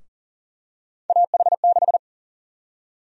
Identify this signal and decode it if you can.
Morse code, heard as a keyed tone, about 40 wpm, 700 Hz. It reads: AH6
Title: USA Call Sign Prefixes - Repeated 2x - 40wpm